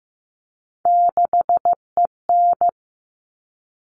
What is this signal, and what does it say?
Morse code, heard as a keyed tone, about 15 wpm, 700 Hz. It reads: 6EN
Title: All Cut Numbers - 3-Digit 15wpm